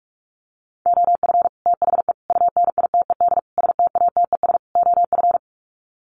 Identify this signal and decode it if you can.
Morse code, heard as a keyed tone, about 44 wpm, 700 Hz. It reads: OF THE UNITED STATES OF
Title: Sets of 5 Words Encore 45wpm